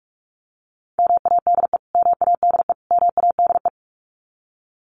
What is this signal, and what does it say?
Morse code, heard as a keyed tone, about 44 wpm, 700 Hz. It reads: MADE MADE MADE
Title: Top 300 Words Repeated 45wpm